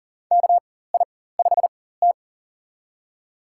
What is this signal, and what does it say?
Morse code, heard as a keyed tone, about 40 wpm, spaced as for 19 wpm, 700 Hz. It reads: KI5T
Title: USA Call Signs - 4x Character Spacing - 40wpm